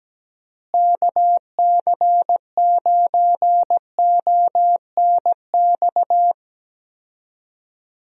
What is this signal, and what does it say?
Morse code, heard as a keyed tone, about 17 wpm, 700 Hz. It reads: KC9ONX
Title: USA Call Signs - 6 Characters - Repeated 3x - 17wpm